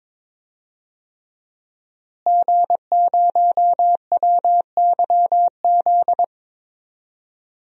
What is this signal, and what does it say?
Morse code, heard as a keyed tone, about 21 wpm, 700 Hz. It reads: G0WYZ
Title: DXCC-100 Call Signs - 5 Characters - Repeated 3x - 22wpm